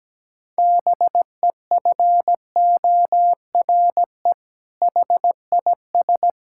Morse code, heard as a keyed tone, about 17 wpm, 700 Hz. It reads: BEFORE HIS
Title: Sentences from Top 1000 Words - Review Part 5 - 17wpm